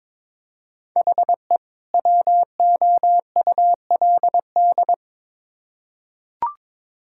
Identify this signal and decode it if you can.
Morse code, heard as a keyed tone, about 22 wpm, 700 Hz. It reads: HE WOULD
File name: Build A Phrase - 2-Word Phrases - 22wpm